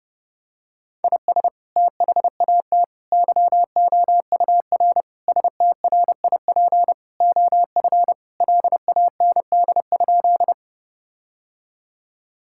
Morse code, heard as a keyed tone, about 30 wpm, 700 Hz. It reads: IS THAT YOUR STRIP OF LAND?
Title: Sentences from Top 1600 Words 30wpm